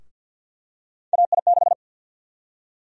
Morse code, heard as a keyed tone, about 53 wpm, 700 Hz. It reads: AI6
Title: USA Call Sign Prefixes - Repeated 2x - 50wpm